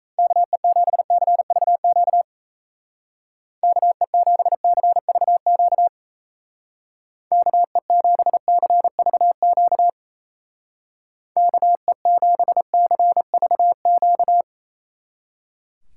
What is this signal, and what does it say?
Morse code, heard as a keyed tone, about 36 wpm, 700 Hz. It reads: KE7CVQ KE7CVQ KE7CVQ KE7CVQ